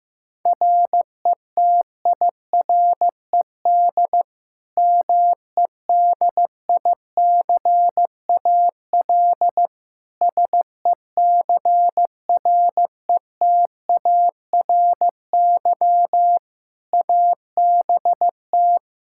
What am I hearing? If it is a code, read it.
Morse code, heard as a keyed tone, about 15 wpm, 700 Hz. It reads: RETIRED MEDICAL SECRETARY ABT
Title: Practice QSO - Exchange 3 - More Info - 15wpm